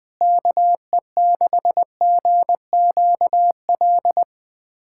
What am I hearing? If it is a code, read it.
Morse code, heard as a keyed tone, about 21 wpm, 700 Hz. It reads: KE6GQL
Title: USA Callsign DE Callsign 20wpm